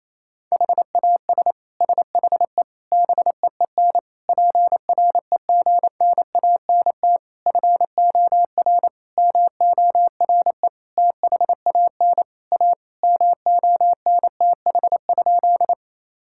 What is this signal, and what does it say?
Morse code, heard as a keyed tone, about 28 wpm, 700 Hz. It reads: HAS SHE BEEN PREGNANT FOR MORE THAN A MONTH?
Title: Sentences from Top 1000 Words - Review Part 6 - 28wpm